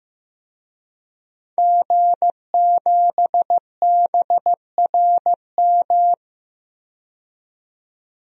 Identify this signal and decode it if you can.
Morse code, heard as a keyed tone, about 15 wpm, 700 Hz. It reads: G7BRM
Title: DXCC-100 Call Signs - 5 Characters - 15wpm